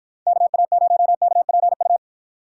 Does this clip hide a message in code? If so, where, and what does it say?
Yes: Morse code from 0.3 to 2.0 s, heard as a keyed tone, about 48 wpm, 700 Hz. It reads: KA0KPU